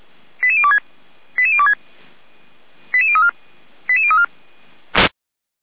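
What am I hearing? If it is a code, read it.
Five-tone selective call (ZVEI), heard as quick runs of five tones, tones of about 70 ms. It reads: 88026 88026 88034 88034
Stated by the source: ZVEI five-tone selective call received on an SDR.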